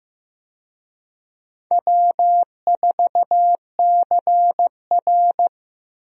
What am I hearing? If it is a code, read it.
Morse code, heard as a keyed tone, about 15 wpm, 700 Hz. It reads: W4CR
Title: USA Call Signs - 4 Characters - 15wpm